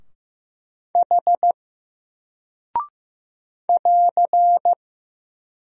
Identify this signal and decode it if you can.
Morse code, heard as a keyed tone, about 15 wpm, 700 Hz. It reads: H <AR>
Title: Single Character (Letter:Number:Punctuation:Prosign) 15wpm